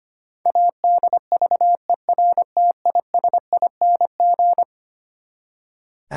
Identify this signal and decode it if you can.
Morse code, heard as a keyed tone, about 25 wpm, 700 Hz. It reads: ADVERTISING
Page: morse-code-ninja-practice-25wpm